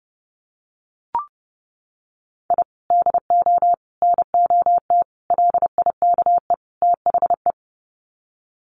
Morse code, heard as a keyed tone, about 30 wpm, 700 Hz. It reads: I DO NOT LIKE THE
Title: Sets of 5 Words Encore 30wpm